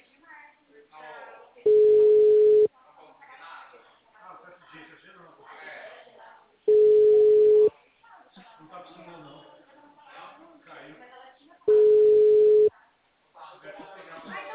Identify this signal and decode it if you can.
Ringback tone (Europe, 425 Hz), with speech.
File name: out- -2004- - - .1677